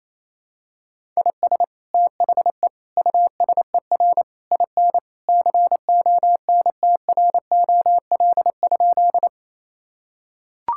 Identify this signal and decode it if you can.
Morse code, heard as a keyed tone, about 28 wpm, 700 Hz. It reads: IS THE USER IN CONTROL?
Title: Sentences from Top 1000 Words - Review Part 5 - 28wpm